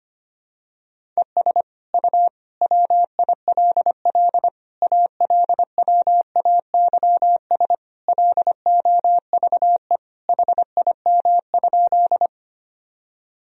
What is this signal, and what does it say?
Morse code, heard as a keyed tone, about 25 wpm, 700 Hz. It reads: ES U WILL ALWAYS LOVE HIM?
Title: Sentences from Top 400 Words and Abbreviations 25wpm